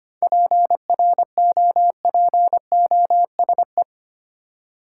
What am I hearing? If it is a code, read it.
Morse code, heard as a keyed tone, about 25 wpm, 700 Hz. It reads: PROPOSE
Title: Seven Letter Words - 25wpm